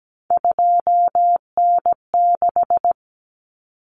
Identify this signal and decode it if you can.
Morse code, heard as a keyed tone, about 17 wpm, 700 Hz. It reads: 2N6
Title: Common Cut Numbers - 3-Digit 17wpm